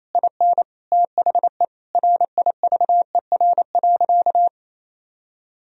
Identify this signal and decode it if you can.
Morse code, heard as a keyed tone, about 28 wpm, 700 Hz. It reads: IN THE RIVER.
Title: Sentences from Top 1000 Words - Review Part 4 - 28wpm